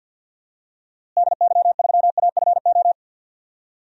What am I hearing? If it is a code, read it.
Morse code, heard as a keyed tone, about 51 wpm, 700 Hz. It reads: DY3AFO
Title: DXCC-100 Call Signs - 6 Characters - Repeated 2x - 50wpm